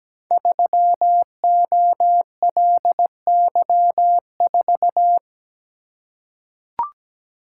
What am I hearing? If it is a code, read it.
Morse code, heard as a keyed tone, about 17 wpm, 700 Hz. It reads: 3OLY4
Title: Code Groups - 5 Letter:Numbers - 17wpm